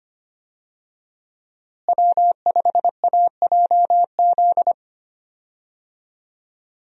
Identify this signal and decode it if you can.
Morse code, heard as a keyed tone, about 25 wpm, 700 Hz. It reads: W5AJZ
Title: USA Call Signs - 5 Characters - Repeated 3x - 25wpm